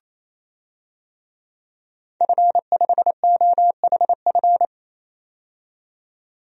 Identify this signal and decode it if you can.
Morse code, heard as a keyed tone, about 28 wpm, 700 Hz. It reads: F5OHF
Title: DXCC-100 Call Signs - 5 Characters - 28wpm